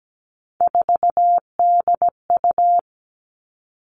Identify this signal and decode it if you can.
Morse code, heard as a keyed tone, about 17 wpm, 700 Hz. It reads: 4DU
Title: All Cut Numbers - 3-Digit 17wpm